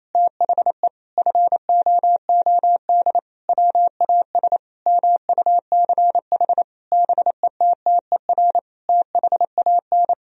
Morse code, heard as a keyed tone, about 28 wpm, 700 Hz. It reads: THE FOOD WAS MUCH BETTER THAN
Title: Sentences from Top 1000 Words - Review Part 2 - 28wpm